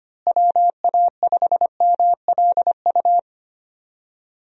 Morse code, heard as a keyed tone, about 25 wpm, 700 Hz. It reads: WA5MLU
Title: USA Call Signs - 6 Characters - Repeated 3x - 25wpm